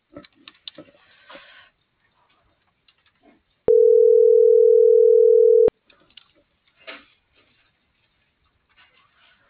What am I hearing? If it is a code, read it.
Ringback tone (North America), 440 + 480 Hz.